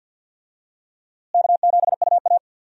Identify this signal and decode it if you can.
Morse code, heard as a keyed tone, about 51 wpm, 700 Hz. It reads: K7UA
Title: USA Call Signs - 4 Characters - Repeated 3x - 50wpm